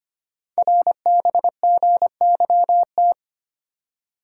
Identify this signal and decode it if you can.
Morse code, heard as a keyed tone, about 25 wpm, 700 Hz. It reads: RBGYT
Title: Code Groups - 5 Letters - 25wpm